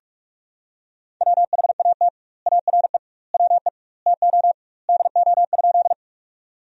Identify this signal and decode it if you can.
Morse code, heard as a keyed tone, about 42 wpm, 700 Hz. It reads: WHAT ARE WE TO DO?
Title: 2x Word-Spacing - Sentences from Top 100 Words - 45wpm